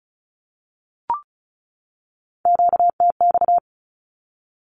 Morse code, heard as a keyed tone, about 34 wpm, 700 Hz. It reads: QTX
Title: Q-Codes 35wpm